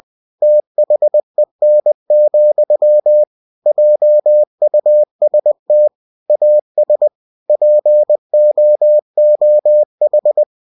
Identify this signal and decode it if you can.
Morse code, heard as a keyed tone, about 20 wpm, 600 Hz. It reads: THEN, JUST AS POOH
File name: cw-books / winnie-the-pooh / winnie-20wpm-page-0037